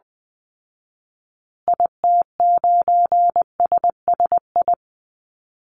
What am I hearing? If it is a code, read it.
Morse code, heard as a keyed tone, about 20 wpm, 700 Hz. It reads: IT9SSI
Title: Call Signs Repeated 2x - 20wpm